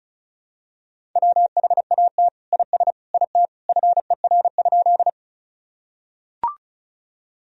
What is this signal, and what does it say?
Morse code, heard as a keyed tone, about 36 wpm, 700 Hz. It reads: WHAT IS IT FER?
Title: Sentences from Top 100 Words and Abbreviations 35wpm